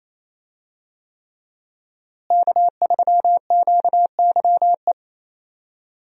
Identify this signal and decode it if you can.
Morse code, heard as a keyed tone, about 28 wpm, 700 Hz. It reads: K3QYE
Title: USA Call Signs - 5 Characters - Repeated 3x - 28wpm